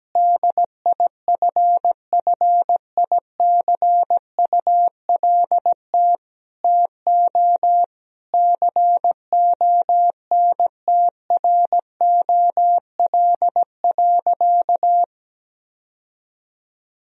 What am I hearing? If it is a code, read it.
Morse code, heard as a keyed tone, about 17 wpm, 700 Hz. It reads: DIFFICULT TO CONTROL.
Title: Sentences from Top 1000 Words - Review Part 4 - 17wpm